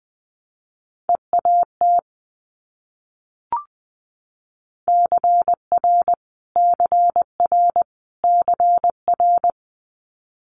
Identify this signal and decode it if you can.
Morse code, heard as a keyed tone, about 20 wpm, 700 Hz. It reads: EAT CR CR CR
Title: Multi-letter Phonemes Repeated 3x 20wpm